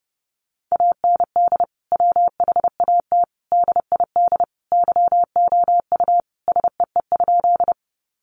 Morse code, heard as a keyed tone, about 30 wpm, 700 Hz. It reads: AND WHAT DID YOU SEE?
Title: Sentences from Top 100 Words 30wpm